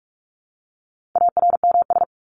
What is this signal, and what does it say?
Morse code, heard as a keyed tone, about 44 wpm, 700 Hz. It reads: ARMS
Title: ES:S Suffix - Top 100 Words with Base Words 45wpm